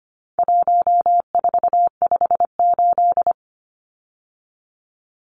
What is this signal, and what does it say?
Morse code, heard as a keyed tone, about 25 wpm, 700 Hz. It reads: 1458
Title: Numbers - 4-Digit 25wpm